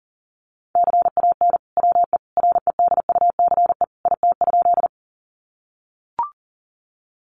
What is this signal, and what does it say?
Morse code, heard as a keyed tone, about 40 wpm, 700 Hz. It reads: CAN WE REDUCE IT?
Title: Sentences from Top 1100 Words 40wpm